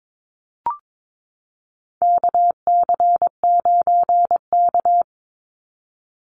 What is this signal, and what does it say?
Morse code, heard as a keyed tone, about 22 wpm, 700 Hz. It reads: KC9K
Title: Call Signs Repeated 2x - 22wpm